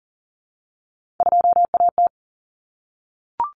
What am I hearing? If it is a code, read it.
Morse code, heard as a keyed tone, about 39 wpm, 700 Hz. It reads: 2AT
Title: Common Cut Numbers - 3-Digit 40wpm